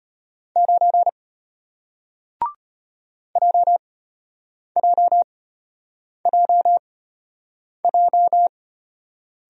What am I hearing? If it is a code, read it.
Morse code, heard as a keyed tone, about 33 wpm, spaced as for 8 wpm, 700 Hz. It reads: 9 JJJJ